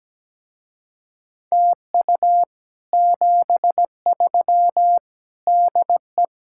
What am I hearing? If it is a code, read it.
Morse code, heard as a keyed tone, about 17 wpm, 700 Hz. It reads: TU 73 DE
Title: Speed-Racing Common QSO Phrases Encore - 17wpm